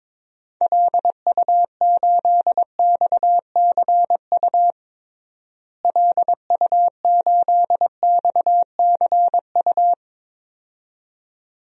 Morse code, heard as a keyed tone, about 22 wpm, 700 Hz. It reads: LU8XCU LU8XCU
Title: DXCC-100 Call Signs - 6 Characters - Repeated 2x - 22wpm